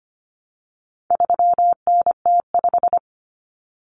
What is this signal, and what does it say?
Morse code, heard as a keyed tone, about 25 wpm, 700 Hz. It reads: 3NT5
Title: Common Cut Numbers - 4-Digit 25wpm